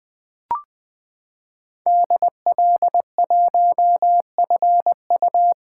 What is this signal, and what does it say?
Morse code, heard as a keyed tone, about 20 wpm, 700 Hz. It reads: DL1FU
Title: 2023 SOTA Chaser Call Signs - 20wpm